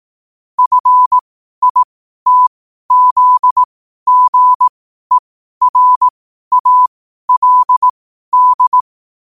Morse code, heard as a keyed tone, about 18 wpm, spaced as for 12 wpm, 1000 Hz. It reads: FITZGERALD
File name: qotdfiles / 12wpm